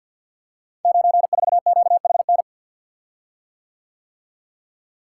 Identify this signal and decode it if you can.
Morse code, heard as a keyed tone, about 47 wpm, 700 Hz. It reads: 94QHN
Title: Code Groups - 5 Letter:Numbers - 50wpm